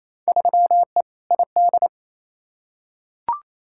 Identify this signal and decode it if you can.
Morse code, heard as a keyed tone, about 28 wpm, 700 Hz. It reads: 3E ID
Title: Field Day Exchanges Encore - 28wpm